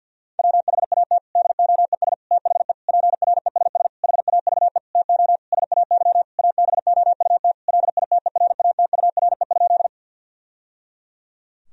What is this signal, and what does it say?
Morse code, heard as a keyed tone, about 49 wpm, 700 Hz. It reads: WHAT DOES THE PRESS HAVE TO SAY ABOUT LITERATURE?
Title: Sentences from Top 1000 Words - Review Part 3 - 50wpm